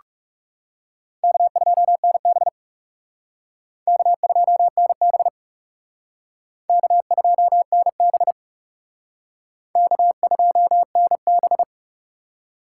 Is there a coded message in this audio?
Yes: Morse code from 1.2 to 11.6 s, heard as a keyed tone, about 38 wpm, 700 Hz. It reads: K2NB K2NB K2NB K2NB